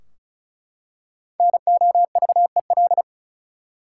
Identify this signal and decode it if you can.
Morse code, heard as a keyed tone, about 36 wpm, 700 Hz. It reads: NOVEL
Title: Five Letter Words - 35wpm